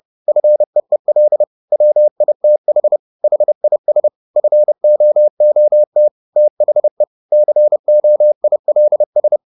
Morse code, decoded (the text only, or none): FEEL WITH HIS FOOT THE COILS